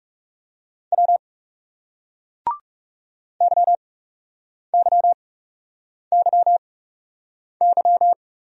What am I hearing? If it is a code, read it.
Morse code, heard as a keyed tone, about 40 wpm, 700 Hz. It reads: W Y Y Y Y